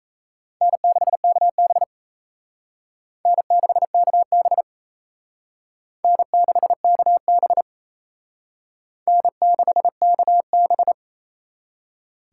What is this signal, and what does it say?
Morse code, heard as a keyed tone, about 34 wpm, 700 Hz. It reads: N6KB N6KB N6KB N6KB
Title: Speed Racing - USA Call Signs 28wpm